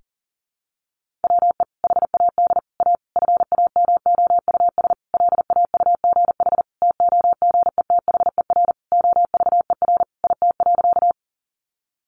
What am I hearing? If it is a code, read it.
Morse code, heard as a keyed tone, about 37 wpm, 700 Hz. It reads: WE HAD A FAMOUS LAUGH TOGETHER OVER IT.
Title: Sentences from Top 1500 Words 40wpm